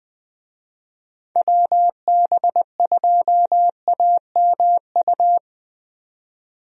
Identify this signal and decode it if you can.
Morse code, heard as a keyed tone, about 20 wpm, 700 Hz. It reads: WB2AMU
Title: USA Call Signs - 6 Characters - 20wpm